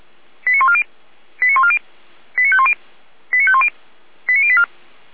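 Five-tone selective call (ZVEI), heard as quick runs of five tones, tones of about 70 ms. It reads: 89140 89140 89510 89510 89084
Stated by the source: ZVEI five-tone call 89052 received on SDR.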